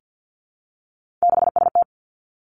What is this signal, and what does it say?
Morse code, heard as a keyed tone, about 50 wpm, 700 Hz. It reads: 6ST
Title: Three Challenging Letter-Number Combinations 50wpm